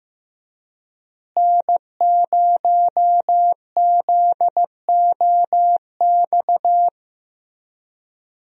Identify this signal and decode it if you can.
Morse code, heard as a keyed tone, about 15 wpm, 700 Hz. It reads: N0ZOX
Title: USA Call Signs - 5 Characters - Repeated 2x - 15wpm